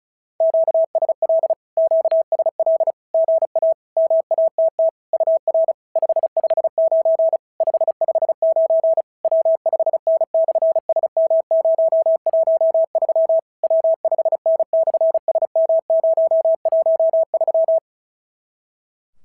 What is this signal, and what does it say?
Morse code, heard as a keyed tone, about 36 wpm, 650 Hz. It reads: QSL QSL GA MATT UR 559 559 W5N/SM013 W5N/SM013
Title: SOTA QSO Practice 35wpm